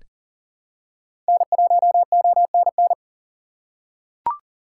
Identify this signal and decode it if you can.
Morse code, heard as a keyed tone, about 38 wpm, 700 Hz. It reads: N1ONN
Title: USA Call Signs - 5 Characters - 3x Character Spacing - 40wpm